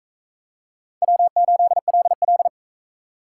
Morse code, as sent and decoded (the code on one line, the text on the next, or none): .-- ---.. .-.. .-..
W8LL